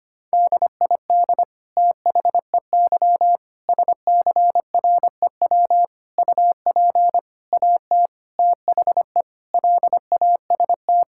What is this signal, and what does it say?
Morse code, heard as a keyed tone, about 25 wpm, 700 Hz. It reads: DID THEY SCREW UP AT THE LAST
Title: Sentences from Top 1600 Words 25wpm